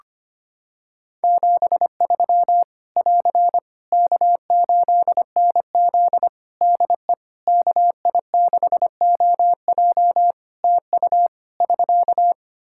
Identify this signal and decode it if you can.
Morse code, heard as a keyed tone, about 25 wpm, 700 Hz. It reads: 73 <AR> K8NZ DE KI6OJ TU <SK>
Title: Practice QSO - Exchange 4 - Ending - 25wpm